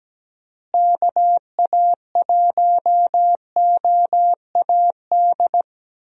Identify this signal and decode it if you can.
Morse code, heard as a keyed tone, about 17 wpm, 700 Hz. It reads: KA1OAD
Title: USA Call Signs - 6 Characters - Repeated 3x - 17wpm